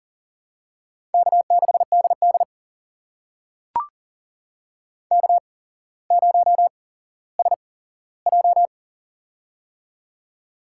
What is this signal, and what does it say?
Morse code, heard as a keyed tone, about 40 wpm, spaced as for 17 wpm, 700 Hz. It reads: K6DD K0SJ
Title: USA Call Signs - 4 Characters - 8x Character Spacing - 40wpm